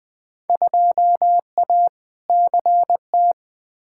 Morse code, heard as a keyed tone, about 20 wpm, 700 Hz. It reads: 2A CT
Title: Field Day Exchanges Encore - 20wpm